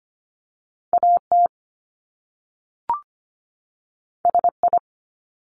Morse code, heard as a keyed tone, about 25 wpm, 700 Hz. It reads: AT SI
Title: Top Two Letter Combinations in Words 25wpm